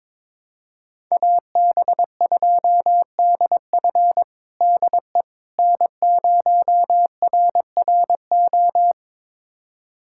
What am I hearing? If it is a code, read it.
Morse code, heard as a keyed tone, about 22 wpm, 700 Hz. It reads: AB2DF DE N0RRO
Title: USA Callsign DE Callsign 22wpm